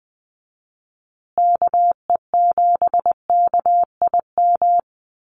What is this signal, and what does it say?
Morse code, heard as a keyed tone, about 20 wpm, 700 Hz. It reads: KE7KIM
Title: USA Call Signs Repeated 3x 20wpm